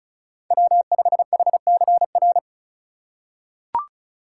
Morse code, heard as a keyed tone, about 36 wpm, 700 Hz. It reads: W5HCR
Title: USA Call Signs - 5 Characters - 3x Character Spacing - 35wpm